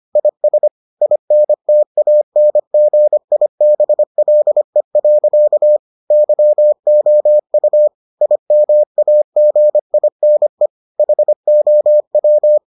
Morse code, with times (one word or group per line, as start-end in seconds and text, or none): 0.2-0.7 IS
1.0-5.8 INTANGIBLE.
6.1-7.9 YOU
8.2-10.7 IMAGINE
11.0-12.6 HOW